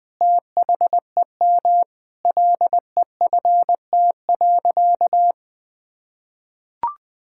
Morse code, heard as a keyed tone, about 20 wpm, 700 Hz. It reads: THEM LEFT.
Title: Sentences with 2-Word Phrases - 20wpm